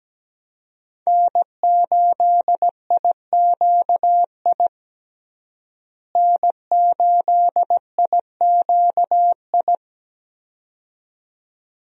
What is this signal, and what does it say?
Morse code, heard as a keyed tone, about 17 wpm, 700 Hz. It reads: N8IQI N8IQI
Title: USA Call Signs - 5 Characters - Repeated 2x - 17wpm